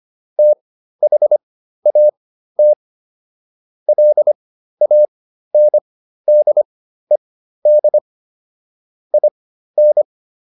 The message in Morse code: THAT LANDED IN

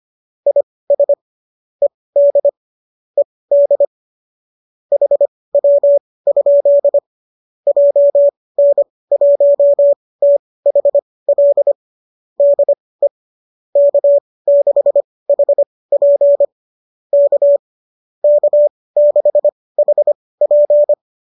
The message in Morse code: IS ED ED HW? JN1THL DE K6HP K K6HP

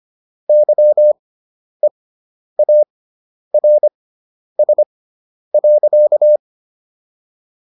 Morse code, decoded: YEARS.

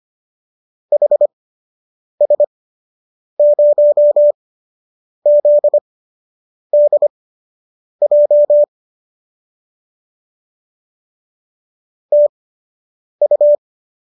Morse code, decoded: HS0ZDJ TU

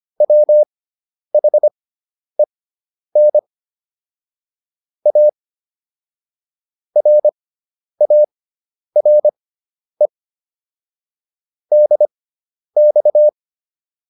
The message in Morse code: WHEN A RARE DX

